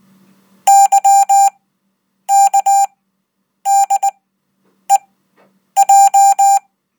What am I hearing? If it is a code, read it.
Morse code, heard as a keyed tone, about 20 wpm, spaced as for 8 wpm, 790 Hz. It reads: YKDEJ
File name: Code